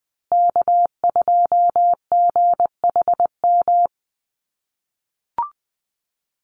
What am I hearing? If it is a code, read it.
Morse code, heard as a keyed tone, about 20 wpm, 700 Hz. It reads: K2GHM
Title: USA Call Signs Repeated 3x 20wpm